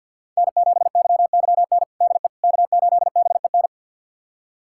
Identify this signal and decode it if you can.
Morse code, heard as a keyed tone, about 52 wpm, 700 Hz. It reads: N7YYN DE K8BEN
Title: USA Callsign DE Callsign - 5 Characters - 50wpm